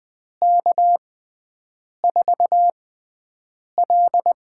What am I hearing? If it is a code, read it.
Morse code, heard as a keyed tone, about 20 wpm, spaced as for 7 wpm, 700 Hz. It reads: K4L
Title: USA Call Signs - 6x Character Spacing - 20wpm